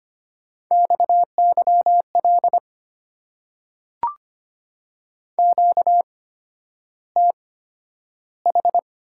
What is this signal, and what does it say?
Morse code, heard as a keyed tone, about 25 wpm, spaced as for 11 wpm, 700 Hz. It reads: XYL QTH
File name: Top 90 QSO Elements - 8x Character Spacing - 25wpm